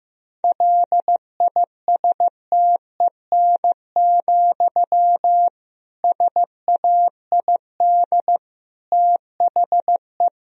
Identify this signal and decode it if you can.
Morse code, heard as a keyed tone, about 15 wpm, 700 Hz. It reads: LISTEN, SAID THE
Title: Sentences from Top 1000 Words - Review Part 3 - 15wpm